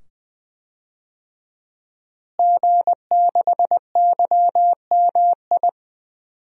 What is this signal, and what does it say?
Morse code, heard as a keyed tone, about 21 wpm, 700 Hz. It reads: G6YMI
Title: DXCC-100 Call Signs - 5 Characters - Repeated 2x - 20wpm